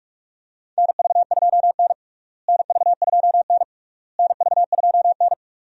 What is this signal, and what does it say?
Morse code, heard as a keyed tone, about 46 wpm, 700 Hz. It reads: NV2N NV2N NV2N
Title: USA Call Signs - 4 Characters - Repeated 3x - 45wpm